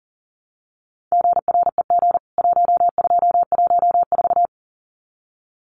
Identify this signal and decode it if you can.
Morse code, heard as a keyed tone, about 40 wpm, 700 Hz. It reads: GREG 1214
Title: CWT Exchange 40wpm